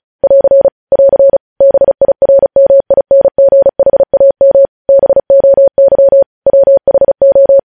Morse code, heard as a keyed tone, about 36 wpm, 550 Hz. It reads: <AR> <AR> BIRMINGHAM BOY WHO
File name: podcast / News headlines-1x-35x35